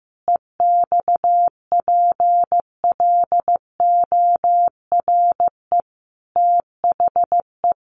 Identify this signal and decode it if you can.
Morse code, heard as a keyed tone, about 15 wpm, 700 Hz. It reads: EXPLORE THE
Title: Sentences from Top 1100 Words 15wpm